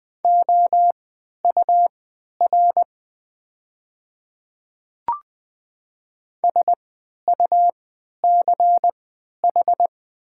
Morse code, heard as a keyed tone, about 20 wpm, spaced as for 11 wpm, 700 Hz. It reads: OUR SUCH
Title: 3x Character Spacing - Top 200 Words - 20wpm